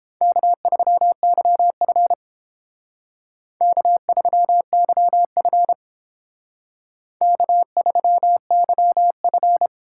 Morse code, heard as a keyed tone, about 30 wpm, 700 Hz. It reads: K3YF K3YF K3YF